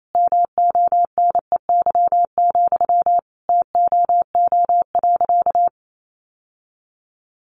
Morse code, heard as a keyed tone, about 28 wpm, 700 Hz. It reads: MONEY, TOO.
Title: Sentences from Top 900 Words 28wpm